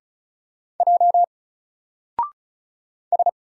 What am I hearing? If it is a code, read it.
Morse code, heard as a keyed tone, about 36 wpm, 700 Hz. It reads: J S
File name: Single-Letter-Number-35wpm